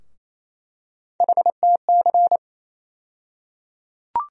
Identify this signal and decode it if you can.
Morse code, heard as a keyed tone, about 28 wpm, 700 Hz. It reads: HTC
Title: USA Call Sign Postfixes - 3 Letters 28wpm